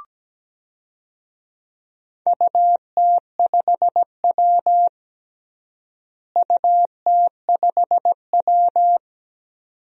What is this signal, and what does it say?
Morse code, heard as a keyed tone, about 17 wpm, 700 Hz. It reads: UT5W UT5W
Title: DXCC-100 Call Signs - 4 Characters - Repeated 2x - 17wpm